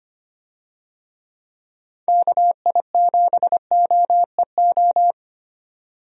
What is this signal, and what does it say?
Morse code, heard as a keyed tone, about 25 wpm, 700 Hz. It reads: KI7OEO
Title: USA Call Signs - 6 Characters - 25wpm